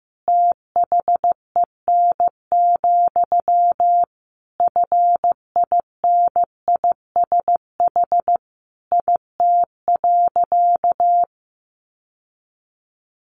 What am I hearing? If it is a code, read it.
Morse code, heard as a keyed tone, about 15 wpm, 700 Hz. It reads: THEN, FINISH IT.
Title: Sentences from Top 1200 Words 15wpm